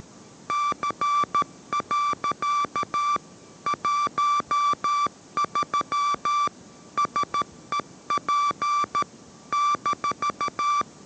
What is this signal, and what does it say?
Morse code, heard as a keyed tone, about 13 wpm, 1170 Hz. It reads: C. 13 SEP -